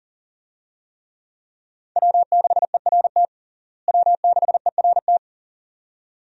Morse code, heard as a keyed tone, about 40 wpm, 700 Hz. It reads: W6ERT W6ERT
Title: USA Call Signs - 5 Characters - Repeated 2x - 40wpm